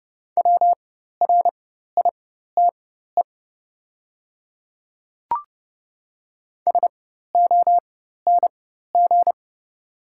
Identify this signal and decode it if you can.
Morse code, heard as a keyed tone, about 30 wpm, spaced as for 14 wpm, 700 Hz. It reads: WRITE SONG
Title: 4x Character Spacing - Top 500 Words - 30wpm